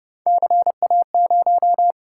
Morse code, heard as a keyed tone, about 28 wpm, 700 Hz. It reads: CA0